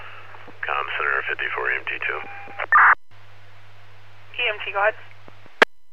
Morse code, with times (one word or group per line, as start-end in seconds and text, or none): none